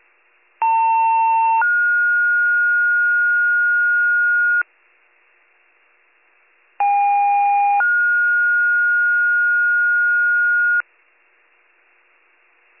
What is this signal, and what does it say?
Two-tone page, heard as a one-second tone, then a three-second tone, about 900 then 1430 Hz, about 810 then 1430 Hz. It is heard from 0.6 to 10.8 s.